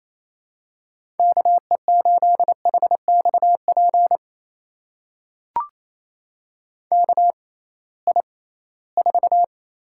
Morse code, heard as a keyed tone, about 27 wpm, spaced as for 18 wpm, 700 Hz. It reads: KE8HXP KI4
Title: USA Call Signs - 6 Characters - 6x Character Spacing - 28wpm